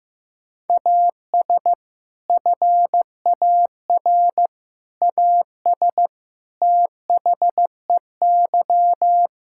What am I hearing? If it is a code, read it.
Morse code, heard as a keyed tone, about 15 wpm, 700 Hz. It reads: AS FAR AS THEY
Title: Sentences from Top 1000 Words - Review Part 6 - 15wpm